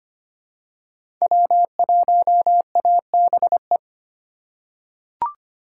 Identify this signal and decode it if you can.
Morse code, heard as a keyed tone, about 25 wpm, 700 Hz. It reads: W1ABE
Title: USA Call Signs - 2x Character Spacing - 25wpm